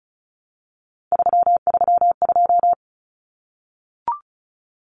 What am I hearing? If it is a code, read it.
Morse code, heard as a keyed tone, about 34 wpm, 700 Hz. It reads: 332
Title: Common Cut Numbers - 3-Digit 35wpm